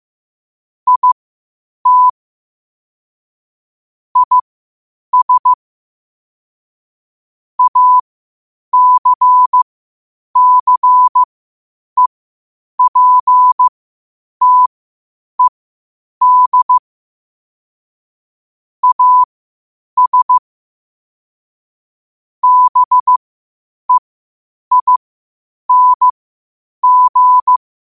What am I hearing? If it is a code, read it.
Morse code, heard as a keyed tone, about 15 wpm, spaced as for 8 wpm, 1000 Hz. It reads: IT IS ACCEPTED AS BEING
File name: qotdfiles / 8wpm